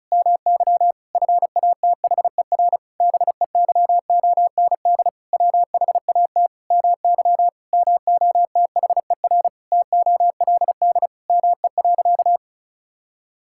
Morse code, heard as a keyed tone, about 36 wpm, 700 Hz. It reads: MY FATHER BEYOND WHAT MY MOTHER TOLD ME.
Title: Sentences from Top 1000 Words - Review Part 6 - 35wpm